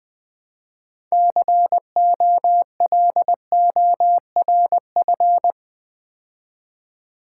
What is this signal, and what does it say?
Morse code, heard as a keyed tone, about 20 wpm, 700 Hz. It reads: COLORF
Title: Build A Word - 2 Letters At A Time - 8-Letter Words - 20wpm